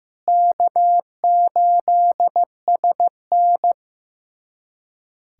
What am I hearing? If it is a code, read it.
Morse code, heard as a keyed tone, about 15 wpm, 700 Hz. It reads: K8SN
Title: USA Callsign DE Callsign - 4 Characters - 15wpm